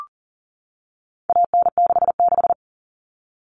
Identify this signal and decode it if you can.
Morse code, heard as a keyed tone, about 37 wpm, 700 Hz. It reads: AN66
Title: All Cut Numbers - 4-Digit 40wpm